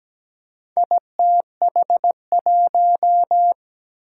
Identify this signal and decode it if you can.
Morse code, heard as a keyed tone, about 17 wpm, 700 Hz. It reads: ITH1
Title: Four Challenging Letter-Number Combinations 17wpm